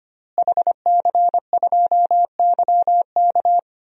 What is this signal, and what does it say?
Morse code, heard as a keyed tone, about 25 wpm, 700 Hz. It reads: HC2YK
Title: DXCC-100 Call Signs - 5 Characters - Repeated 2x - 25wpm